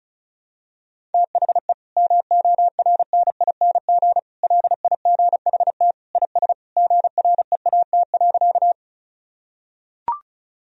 Morse code, heard as a keyed tone, about 34 wpm, 700 Hz. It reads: THE MORNING LIGHT IS GREAT.
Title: Sentences from Top 1000 Words - Review Part 1 - 35wpm